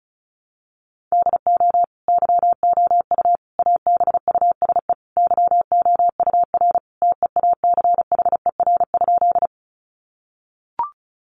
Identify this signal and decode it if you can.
Morse code, heard as a keyed tone, about 34 wpm, 700 Hz. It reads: DO YOU ABUSE YOUR TEACHER?
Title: Sentences from Top 1300 Words 35wpm